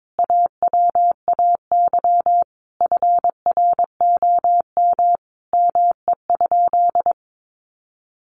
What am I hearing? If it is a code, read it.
Morse code, heard as a keyed tone, about 22 wpm, 700 Hz. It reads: AWAY FROM ME?
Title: Sentences from Top 1500 Words 22wpm